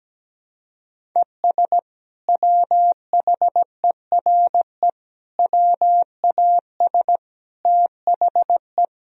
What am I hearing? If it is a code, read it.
Morse code, heard as a keyed tone, about 17 wpm, 700 Hz. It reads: ES WHERE WAS THE
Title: Sentences from Top 200 Words and Abbreviations 17wpm